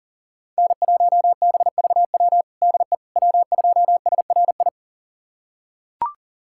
Morse code, heard as a keyed tone, about 40 wpm, 700 Hz. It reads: N1BVW DE W2SRI
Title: USA Callsign DE Callsign - 5 Characters - 40wpm